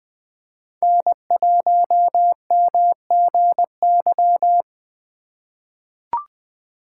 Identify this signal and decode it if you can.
Morse code, heard as a keyed tone, about 21 wpm, 700 Hz. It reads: N1MGY
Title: USA Call Signs - 3x Character Spacing - 20wpm